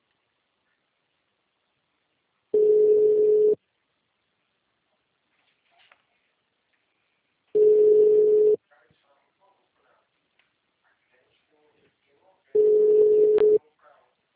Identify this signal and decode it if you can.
Ringback tone (Europe, 425 Hz), with speech.